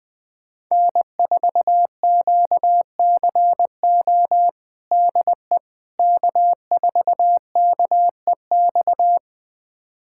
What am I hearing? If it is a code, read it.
Morse code, heard as a keyed tone, about 20 wpm, 700 Hz. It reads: N4QCO DE K4KEX
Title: USA Callsign DE Callsign 20wpm